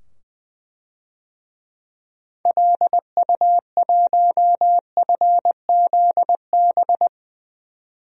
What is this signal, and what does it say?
Morse code, heard as a keyed tone, about 20 wpm, 700 Hz. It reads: LU1FZB
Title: DXCC-100 Call Signs - 6 Characters - 20wpm